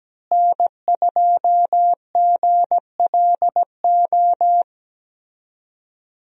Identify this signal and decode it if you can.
Morse code, heard as a keyed tone, about 17 wpm, 700 Hz. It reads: N2GLO
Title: USA Call Signs - 5 Characters - Repeated 2x - 17wpm